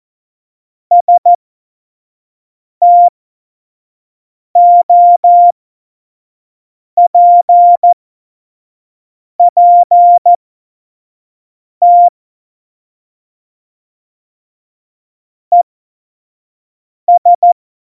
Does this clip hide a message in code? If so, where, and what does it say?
Yes: Morse code from 0.9 to 17.5 s, heard as a keyed tone, about 14 wpm, spaced as for 5 wpm, 700 Hz. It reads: STOPPT ES